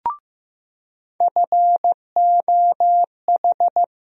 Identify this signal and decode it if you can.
Morse code, heard as a keyed tone, about 15 wpm, 700 Hz. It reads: FOH
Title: USA Call Sign Postfixes - 3 Letters 15wpm